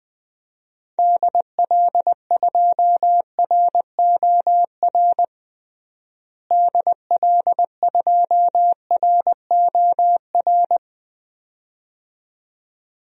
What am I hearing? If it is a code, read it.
Morse code, heard as a keyed tone, about 20 wpm, 700 Hz. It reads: DL2ROR DL2ROR
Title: DXCC-100 Call Signs - 6 Characters - Repeated 2x - 20wpm